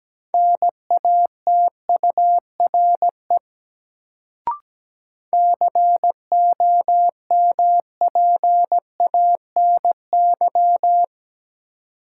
Morse code, heard as a keyed tone, about 17 wpm, 700 Hz. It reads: NATURE COMPANY
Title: Sentences from Top 600 Words - New Word Drill - 17wpm